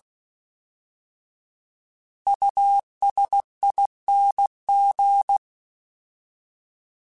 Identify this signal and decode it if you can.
Morse code, heard as a keyed tone, about 16 wpm, 800 Hz. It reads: USING
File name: Alphabetical / Alphabetical mp3s / Us @15